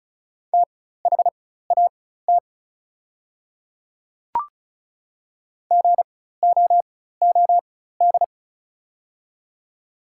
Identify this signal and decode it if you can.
Morse code, heard as a keyed tone, about 36 wpm, spaced as for 16 wpm, 700 Hz. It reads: THAT GOOD
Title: 4x Character Spacing - Top 300 Words - 35wpm